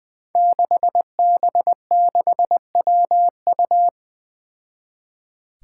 Morse code, heard as a keyed tone, about 20 wpm, 700 Hz. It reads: 6B6WU
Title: Five Challenging Letter-Number Combinations 20wpm